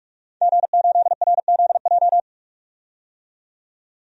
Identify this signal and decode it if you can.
Morse code, heard as a keyed tone, about 44 wpm, 700 Hz. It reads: G8RZJ
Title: DXCC-100 Call Signs - 5 Characters - 45wpm